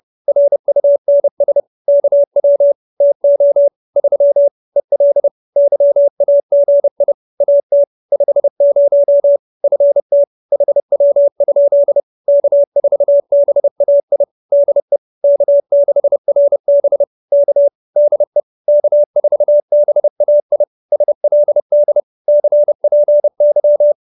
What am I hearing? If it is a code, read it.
Morse code, heard as a keyed tone, about 30 wpm, 570 Hz. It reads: RUNS KW TO 3 EL YAGI AT 50 FT HW? K4BAI DE K6RB K DE K4BAI SLD CPY